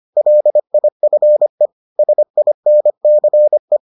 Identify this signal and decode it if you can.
Morse code, heard as a keyed tone, about 25 wpm, 600 Hz. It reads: LIFE SINCE